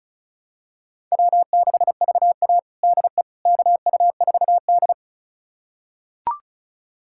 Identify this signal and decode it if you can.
Morse code, heard as a keyed tone, about 34 wpm, 700 Hz. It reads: W6VA DE KU4D
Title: USA Callsign DE Callsign - 4 Characters - 35wpm